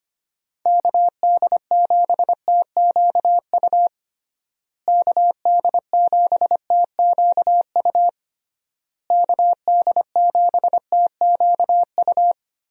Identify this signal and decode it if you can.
Morse code, heard as a keyed tone, about 25 wpm, 700 Hz. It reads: KD7TQU KD7TQU KD7TQU
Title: USA Call Signs - 6 Characters - Repeated 3x - 25wpm